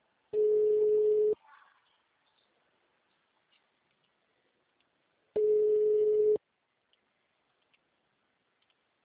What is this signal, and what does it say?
Ringback tone (Europe, 425 Hz).
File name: out- -2034- - - .16